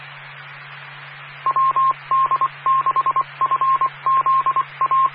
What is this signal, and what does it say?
Morse code, heard as a keyed tone, about 25 wpm, 1060 Hz, with radio noise. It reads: WD6FZA
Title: Claremont, CA Police, Amateur Radio